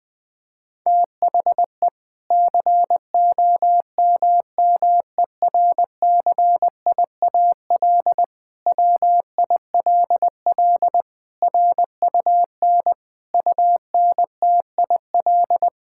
Morse code, heard as a keyed tone, about 21 wpm, 700 Hz. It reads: THE COMMERCIAL WILL RUN UNTIL
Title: Sentences from Top 1000 Words - Review Part 5 - 20wpm